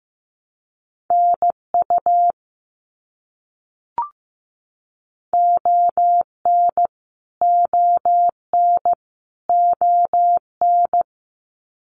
Morse code, heard as a keyed tone, about 15 wpm, 700 Hz. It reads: NU ON ON ON
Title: Canadian Provinces and Territories - Repeated 3x - 15wpm